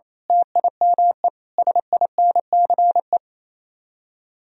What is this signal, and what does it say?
Morse code, heard as a keyed tone, about 27 wpm, 700 Hz. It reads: TIME SINCE
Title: Speed-Racing - Sets of 5 Words 28wpm